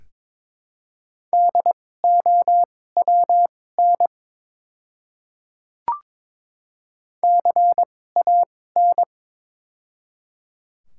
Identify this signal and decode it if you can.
Morse code, heard as a keyed tone, about 22 wpm, spaced as for 16 wpm, 700 Hz. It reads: DOWN CAN
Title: 2x Character Spacing - Top 200 Words - 22wpm